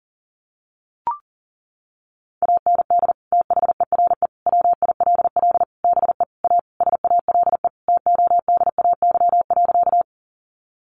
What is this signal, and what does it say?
Morse code, heard as a keyed tone, about 40 wpm, 700 Hz. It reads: AND THERE WILL BE A SALE TODAY.
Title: Sentences from Top 1400 Words 40wpm